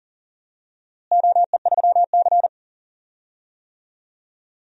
Morse code, heard as a keyed tone, about 40 wpm, 700 Hz. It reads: OE3C
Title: DXCC-100 Call Signs - 4 Characters - Repeated 2x - 40wpm